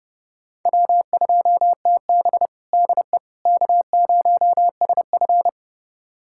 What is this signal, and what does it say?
Morse code, heard as a keyed tone, about 30 wpm, 700 Hz. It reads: W2TB DE K0SF
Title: USA Callsign DE Callsign - 4 Characters - 30wpm